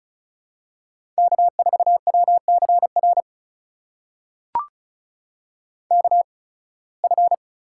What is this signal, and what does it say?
Morse code, heard as a keyed tone, about 36 wpm, spaced as for 23 wpm, 700 Hz. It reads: K4WCR KF
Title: USA Call Signs - 5 Characters - 8x Character Spacing - 35wpm